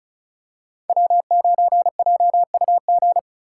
Morse code, heard as a keyed tone, about 36 wpm, 700 Hz. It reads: W9JUG
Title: K1USN SST QSO Practice - 35wpm